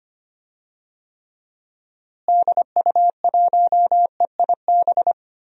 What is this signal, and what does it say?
Morse code, heard as a keyed tone, about 25 wpm, 700 Hz. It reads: DU1EIB
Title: DXCC-100 Call Signs - 6 Characters - Repeated 3x - 25wpm